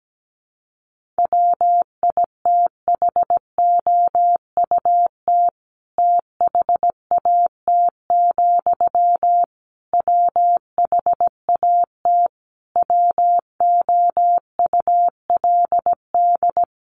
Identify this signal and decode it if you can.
Morse code, heard as a keyed tone, about 17 wpm, 700 Hz. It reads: WITHOUT THAT, WHAT WOULD
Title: Sentences from Top 600 Words 17wpm